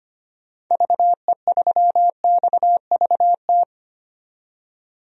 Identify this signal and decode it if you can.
Morse code, heard as a keyed tone, about 25 wpm, 700 Hz. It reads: VE3XVT
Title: DXCC-100 Call Signs - 6 Characters - 25wpm